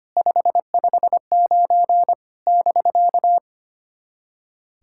Morse code, heard as a keyed tone, about 25 wpm, 700 Hz. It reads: 559 <BK>
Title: SOTA QSO Practice 25wpm